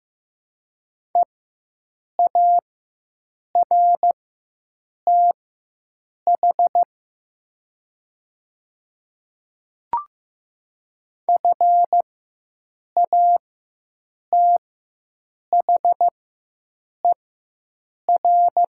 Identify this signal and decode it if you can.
Morse code, heard as a keyed tone, about 15 wpm, spaced as for 7 wpm, 700 Hz. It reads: EARTH FATHER
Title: 4x Character Spacing - Top 200 Words - 15wpm